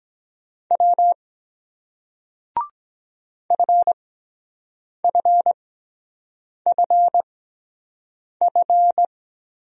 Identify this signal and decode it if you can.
Morse code, heard as a keyed tone, about 23 wpm, spaced as for 7 wpm, 700 Hz. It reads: W FFFF